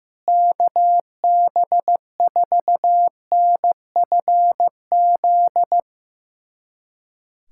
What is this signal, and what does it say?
Morse code, heard as a keyed tone, about 15 wpm, 700 Hz. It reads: KB4NFZ
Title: USA Callsign DE Callsign 15wpm